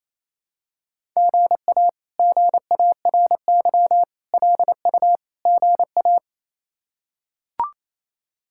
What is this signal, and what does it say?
Morse code, heard as a keyed tone, about 27 wpm, 700 Hz. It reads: GA GARY LU GA
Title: K1USN SST QSO Practice - 28wpm